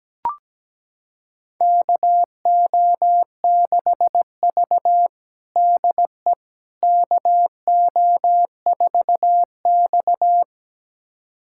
Morse code, heard as a keyed tone, about 17 wpm, 700 Hz. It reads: KO6V DE KO4X
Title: USA Callsign DE Callsign - 4 Characters - 17wpm